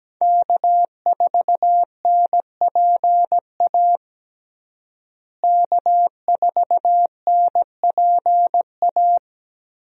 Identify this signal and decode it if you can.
Morse code, heard as a keyed tone, about 17 wpm, 700 Hz. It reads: K4NPA K4NPA
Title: USA Call Signs - 5 Characters - Repeated 3x - 17wpm